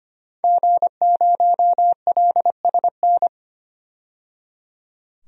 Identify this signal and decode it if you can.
Morse code, heard as a keyed tone, about 25 wpm, 700 Hz. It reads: G0LSN
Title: DXCC-100 Call Signs - 5 Characters - 25wpm